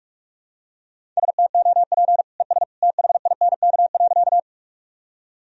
Morse code, heard as a keyed tone, about 46 wpm, 700 Hz. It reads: STOP ES THINK.
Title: Sentences from Top 300 Words and Abbreviations 45wpm